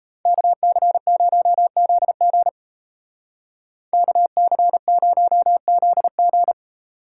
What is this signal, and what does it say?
Morse code, heard as a keyed tone, about 33 wpm, 700 Hz. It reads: KC0ZG KC0ZG
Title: Speed Racing - USA 2x2 Call Signs 28wpm